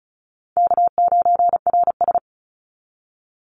Morse code, heard as a keyed tone, about 34 wpm, 700 Hz. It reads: K9RS
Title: Call Signs - 35wpm